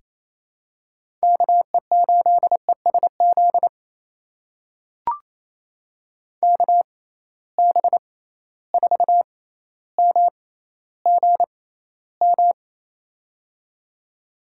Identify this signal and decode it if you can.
Morse code, heard as a keyed tone, about 28 wpm, spaced as for 14 wpm, 700 Hz. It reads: KE8ESZ KB4MGM
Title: USA Call Signs - 6 Characters - 6x Character Spacing - 28wpm